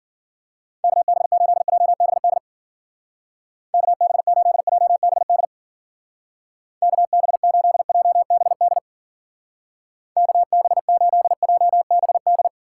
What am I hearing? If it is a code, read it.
Morse code, heard as a keyed tone, about 52 wpm, 700 Hz. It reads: KB8JBD KB8JBD KB8JBD KB8JBD